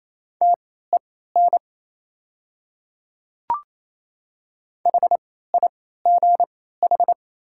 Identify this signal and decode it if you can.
Morse code, heard as a keyed tone, about 28 wpm, spaced as for 16 wpm, 700 Hz. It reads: TEN HIGH
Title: 3x Character Spacing - Top 500 Words - 28wpm